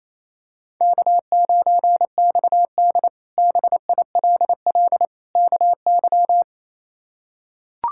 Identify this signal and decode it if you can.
Morse code, heard as a keyed tone, about 28 wpm, 700 Hz. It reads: K9XD BILL KY
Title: K1USN SST QSO Practice - 28wpm